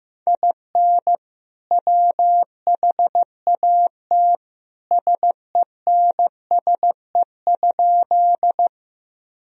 Morse code, heard as a keyed tone, about 16 wpm, 700 Hz. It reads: IN WHAT SENSE?
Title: Sentences from Top 1000 Words - Review Part 4 - 15wpm